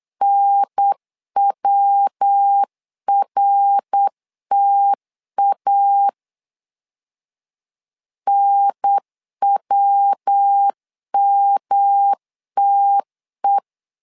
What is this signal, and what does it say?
Morse code, heard as a keyed tone, about 8 wpm, 800 Hz. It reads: NWRTA NWMTE
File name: LEZIONI / 3 MWRO / 3 MWR 8wpm 2444